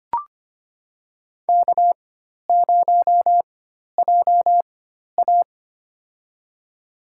Morse code, heard as a keyed tone, about 25 wpm, spaced as for 12 wpm, 700 Hz. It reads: K0JA
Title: USA Call Signs - 4 Characters - 4x Character Spacing - 25wpm